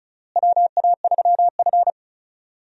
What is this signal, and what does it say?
Morse code, heard as a keyed tone, about 36 wpm, 700 Hz. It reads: WA3F
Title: USA Call Signs - 4 Characters - Repeated 3x - 35wpm